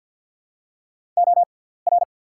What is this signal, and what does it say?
Morse code, heard as a keyed tone, about 48 wpm, spaced as for 17 wpm, 700 Hz. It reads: OR
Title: 6x Character Spacing - Top 300 Words - 50wpm